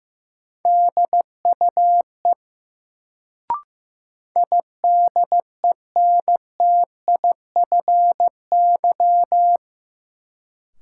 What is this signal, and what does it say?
Morse code, heard as a keyed tone, about 15 wpm, 700 Hz. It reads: DUE IDENTIFY
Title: Sentences from Top 900 Words - New Word Drill - 15wpm